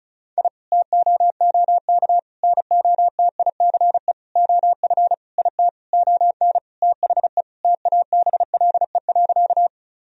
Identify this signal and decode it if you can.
Morse code, heard as a keyed tone, about 36 wpm, 700 Hz. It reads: I TOOK NOTICE OF IT ON THE TABLE.
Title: Sentences from Top 1000 Words - Review Part 2 - 35wpm